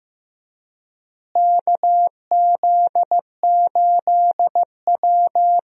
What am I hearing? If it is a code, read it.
Morse code, heard as a keyed tone, about 15 wpm, 700 Hz. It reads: KZ8W
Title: USA Call Signs - 4 Characters - Repeated 3x - 15wpm